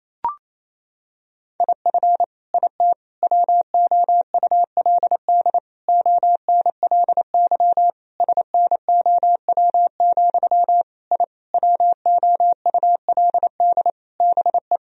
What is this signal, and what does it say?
Morse code, heard as a keyed tone, about 28 wpm, 700 Hz. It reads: IF IT WOULD ONLY SNOW, I WOULD BE
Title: Sentences from Top 1000 Words - Review Part 3 - 28wpm